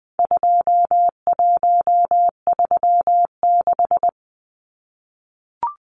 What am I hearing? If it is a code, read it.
Morse code, heard as a keyed tone, about 20 wpm, 700 Hz. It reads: 2136
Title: Numbers - 4-Digit 20wpm